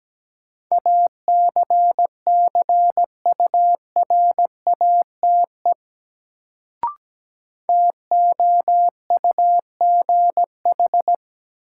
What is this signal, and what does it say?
Morse code, heard as a keyed tone, about 17 wpm, 700 Hz. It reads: ACCURATE TOUGH
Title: Sentences from Top 1200 Words - New Word Drill Alt - 17wpm